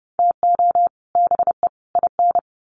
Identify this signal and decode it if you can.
Morse code, heard as a keyed tone, about 30 wpm, 700 Hz. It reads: TO BE IN
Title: Sets of 3 Words 30wpm